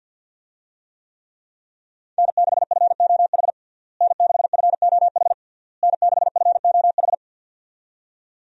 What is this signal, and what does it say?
Morse code, heard as a keyed tone, about 52 wpm, 700 Hz. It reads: N6FOH N6FOH N6FOH
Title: USA Call Signs - 5 Characters - Repeated 3x - 50wpm